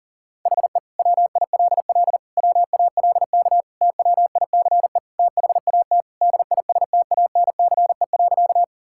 Morse code, heard as a keyed tone, about 40 wpm, 700 Hz. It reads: HE WILL WALK TWICE THAT DISTANCE.
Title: Sentences from Top 1600 Words 40wpm